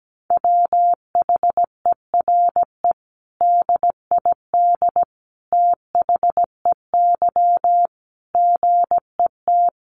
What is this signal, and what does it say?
Morse code, heard as a keyed tone, about 17 wpm, 700 Hz. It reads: WHERE DID THEY GET
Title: Sentences from Top 400 Words 17wpm